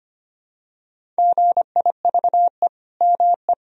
Morse code, heard as a keyed tone, about 25 wpm, 700 Hz. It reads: GIVE ME
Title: Sentences from Top 1000 Words - Review Part 1 - 25wpm